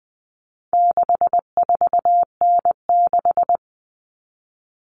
Morse code, heard as a keyed tone, about 20 wpm, 700 Hz. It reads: 64N6
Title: All Cut Numbers - 4-Digit 20wpm